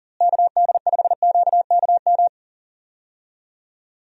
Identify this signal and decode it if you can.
Morse code, heard as a keyed tone, about 37 wpm, 700 Hz. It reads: KD5QKM
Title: USA Call Signs - 6 Characters - Repeated 3x - 40wpm